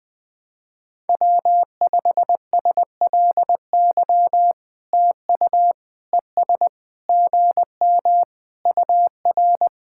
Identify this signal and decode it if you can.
Morse code, heard as a keyed tone, about 20 wpm, 700 Hz. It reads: W5SLY TU ES GM UR
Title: POTA QSO Practice - 20wpm